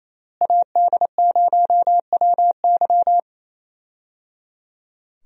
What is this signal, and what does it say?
Morse code, heard as a keyed tone, about 27 wpm, 700 Hz. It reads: AD0WY
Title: USA Call Signs - 5 Characters - Repeated 3x - 28wpm